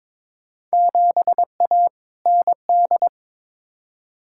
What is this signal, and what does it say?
Morse code, heard as a keyed tone, about 22 wpm, 700 Hz. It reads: 7A ND
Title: Field Day Exchanges Encore - 22wpm